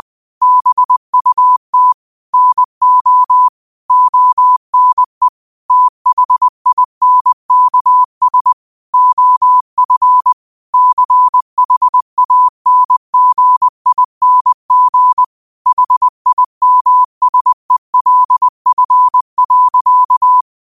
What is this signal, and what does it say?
Morse code, heard as a keyed tone, about 20 wpm, 1000 Hz. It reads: BUT NO ONE THINKS OF CHANGING HIMSELF.